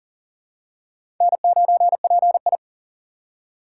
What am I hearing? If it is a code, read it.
Morse code, heard as a keyed tone, about 40 wpm, 700 Hz. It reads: N9PI
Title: USA Call Signs - 2x Character Spacing - 40wpm